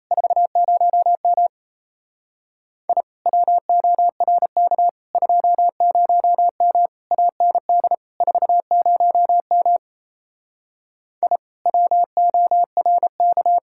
Morse code, heard as a keyed tone, about 37 wpm, 700 Hz. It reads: 40M I WORK 20M AND 40M I WORK